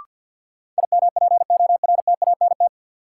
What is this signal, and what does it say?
Morse code, heard as a keyed tone, about 53 wpm, 700 Hz. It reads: IMPORTANT
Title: Sentences from Top 600 Words - New Word Drill - 50wpm